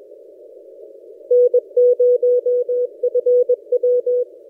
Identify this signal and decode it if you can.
Morse code, heard as a keyed tone, about 21 wpm, 490 Hz. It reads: N0FW
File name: posts / a35jr-12cw- -2336z